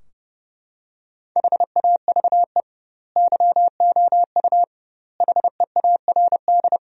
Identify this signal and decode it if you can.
Morse code, heard as a keyed tone, about 30 wpm, 700 Hz. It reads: HAVE YOU HEARD
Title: 2x Word-Spacing - Sentences from Top 400 Words - 30wpm